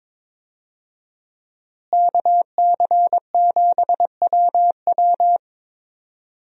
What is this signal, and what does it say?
Morse code, heard as a keyed tone, about 22 wpm, 700 Hz. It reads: KC7WW
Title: USA Call Signs - 5 Characters - Repeated 3x - 22wpm